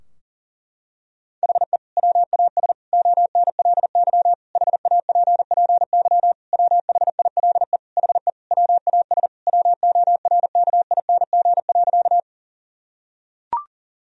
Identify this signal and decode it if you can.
Morse code, heard as a keyed tone, about 40 wpm, 700 Hz. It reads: HE WAS ONLY HAPPY WHILE HE WAS WORKING.
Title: Sentences from Top 1000 Words - Review Part 4 - 40wpm